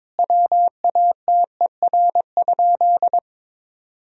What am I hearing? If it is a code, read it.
Morse code, heard as a keyed tone, about 22 wpm, 700 Hz. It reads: WATER?
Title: Sentences from Top 1000 Words - Review Part 3 - 22wpm